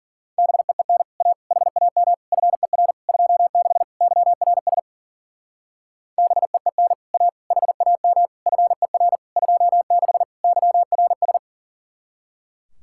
Morse code, heard as a keyed tone, about 50 wpm, 700 Hz. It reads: BEEN A HAM FER 26 YRS BEEN A HAM FER 26 YRS
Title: Speed-Racing Common QSO Phrases Encore - 40wpm